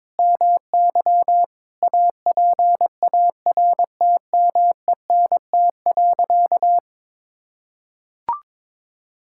Morse code, heard as a keyed tone, about 22 wpm, 700 Hz. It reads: MY APARTMENT.
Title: Sentences from Top 1000 Words - Review Part 6 - 22wpm